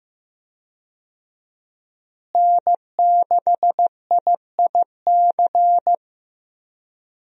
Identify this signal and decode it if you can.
Morse code, heard as a keyed tone, about 16 wpm, 700 Hz. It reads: N6IIC
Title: USA Call Signs - 5 Characters - Repeated 3x - 15wpm